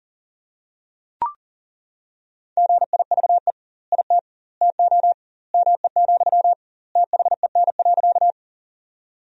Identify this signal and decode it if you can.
Morse code, heard as a keyed tone, about 37 wpm, 700 Hz. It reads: GIVE IT TO ME, THEN.
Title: 2x Word-Spacing - Sentences from Top 200 Words - 40wpm